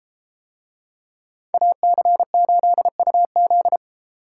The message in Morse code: AC8UZ

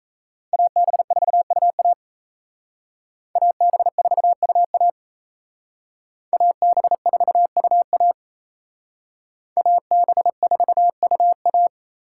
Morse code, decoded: AB4UA AB4UA AB4UA AB4UA

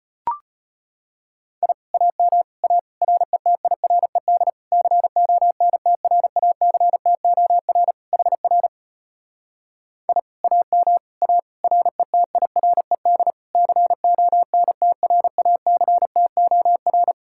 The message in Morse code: I AM A RETIRED CONTRACTOR HR I AM A RETIRED CONTRACTOR

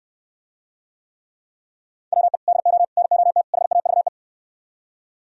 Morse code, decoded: WE DO NOT HAVE

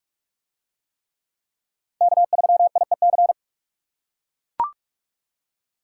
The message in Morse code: K3IEC

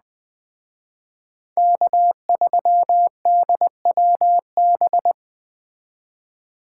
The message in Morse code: K3DWB

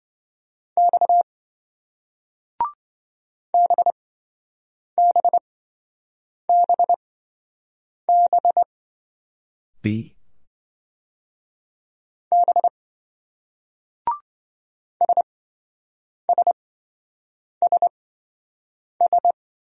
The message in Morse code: X BBBB B SSSS

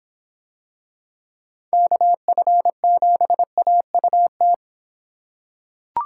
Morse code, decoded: KF7AUT

